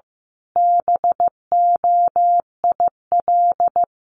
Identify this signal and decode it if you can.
Morse code, heard as a keyed tone, about 15 wpm, 700 Hz. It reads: BOIL
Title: Sets of 5 Words Encore 15wpm